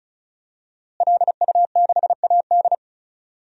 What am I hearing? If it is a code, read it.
Morse code, heard as a keyed tone, about 35 wpm, 700 Hz. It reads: LU6AD
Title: DXCC-100 Call Signs - 5 Characters - Repeated 3x - 35wpm